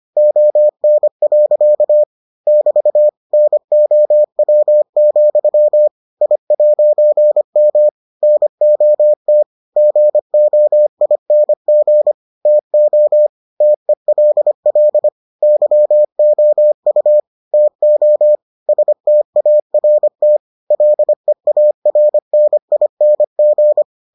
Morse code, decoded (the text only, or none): ON. <BT> NOW, I'M NOT GOING TO TELL YOU TO START LEARNING